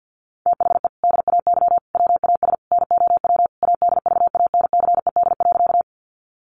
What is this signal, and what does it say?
Morse code, heard as a keyed tone, about 48 wpm, 700 Hz. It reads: THE DAY WAS NOW ADVANCED.
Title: Sentences from Top 1200 Words 50wpm